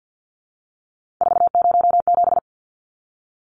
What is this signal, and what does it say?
Morse code, heard as a keyed tone, about 50 wpm, 700 Hz. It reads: 407
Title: Numbers - 3-Digit 50wpm